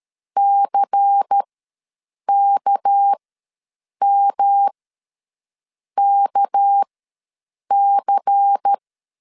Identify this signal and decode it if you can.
Morse code, heard as a keyed tone, about 13 wpm, spaced as for 7 wpm, 800 Hz. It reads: CKMKC
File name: LEZIONI / 6 PKX0 / 6 PK ws 12wpm 3116